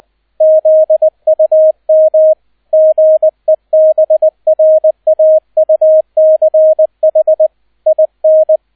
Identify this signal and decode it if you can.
Morse code, heard as a keyed tone, about 20 wpm, 630 Hz, with radio noise. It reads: ZUM GEBRAUCH IN